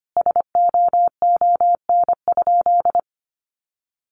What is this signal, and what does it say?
Morse code, heard as a keyed tone, about 25 wpm, 700 Hz. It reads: SOON?
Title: Sentences from Top 400 Words 25wpm